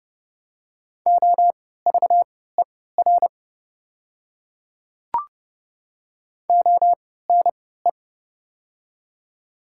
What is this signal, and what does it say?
Morse code, heard as a keyed tone, about 30 wpm, spaced as for 17 wpm, 700 Hz. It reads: OVER ONE
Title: 3x Character Spacing - Top 300 Words - 30wpm